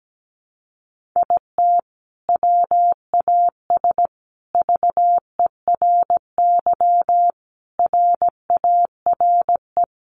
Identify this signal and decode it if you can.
Morse code, heard as a keyed tone, about 17 wpm, 700 Hz. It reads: IT WAS VERY RARE
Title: Sentences from Top 1500 Words 17wpm